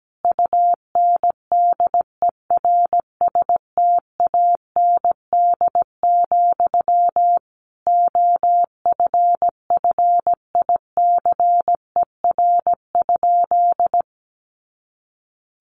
Morse code, heard as a keyed tone, about 17 wpm, 700 Hz. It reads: UNDERSTAND, OFFICER?
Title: Sentences from Top 1200 Words 17wpm